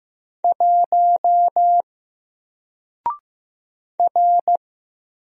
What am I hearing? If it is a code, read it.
Morse code, heard as a keyed tone, about 15 wpm, 700 Hz. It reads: 1 R
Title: Single Letter-Number Alt 15wpm